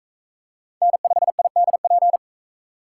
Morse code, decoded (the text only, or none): N5IDP